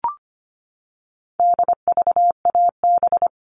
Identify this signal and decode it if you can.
Morse code, heard as a keyed tone, about 25 wpm, 700 Hz. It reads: DVAB
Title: All Cut Numbers - 4-Digit 25wpm